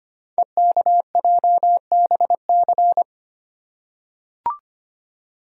Morse code, heard as a keyed tone, about 25 wpm, 700 Hz. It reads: EKJBC
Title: Code Groups - 5 Letters - 25wpm